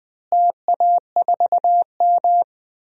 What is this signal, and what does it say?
Morse code, heard as a keyed tone, about 20 wpm, 700 Hz. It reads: TA4M
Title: DXCC-100 Call Signs - 4 Characters - Repeated 3x - 20wpm